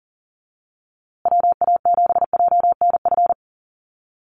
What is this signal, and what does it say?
Morse code, heard as a keyed tone, about 40 wpm, 700 Hz. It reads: WA7JNF
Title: USA Call Signs Repeated 3x 40wpm